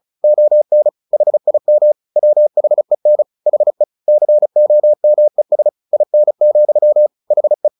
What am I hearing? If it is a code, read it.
Morse code, heard as a keyed tone, about 34 wpm, 600 Hz. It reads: ON HIM WHEN HE COMES IN, HE
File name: the old man and the sea 2 35wpm